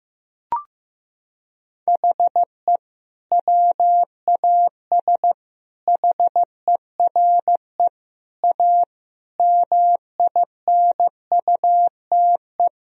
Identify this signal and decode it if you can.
Morse code, heard as a keyed tone, about 15 wpm, 700 Hz. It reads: HE WAS HERE A MINUTE
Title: Sentences from Top 1000 Words - Review Part 3 - 15wpm